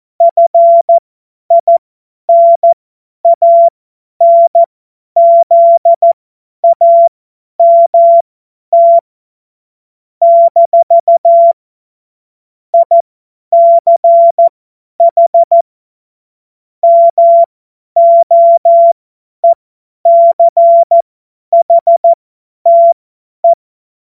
Morse code, heard as a keyed tone, about 14 wpm, spaced as for 10 wpm, 680 Hz. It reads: FINANZAMT - ICH MOECHTE
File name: Witz-02-September-2025-50-F70